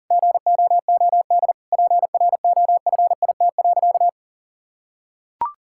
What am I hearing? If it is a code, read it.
Morse code, heard as a keyed tone, about 37 wpm, 700 Hz. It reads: GOOD PROFIT.
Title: Sentences from Top 1600 Words 40wpm